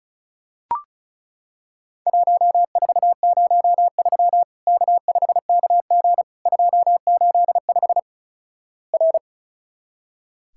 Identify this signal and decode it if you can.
Morse code, heard as a keyed tone, about 36 wpm, 700 Hz. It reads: 1403 K5KG 285 R
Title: Worked All Europe DX Contest - QTC Exchange - 35wpm